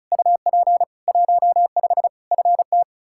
Morse code, heard as a keyed tone, about 36 wpm, 700 Hz. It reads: UP 15 FT
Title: Practice QSO - Exchange 2 - Info - 35wpm